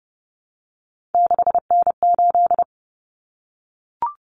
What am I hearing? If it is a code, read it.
Morse code, heard as a keyed tone, about 30 wpm, 700 Hz. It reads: 6N8
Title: Common Cut Numbers - 3-Digit 30wpm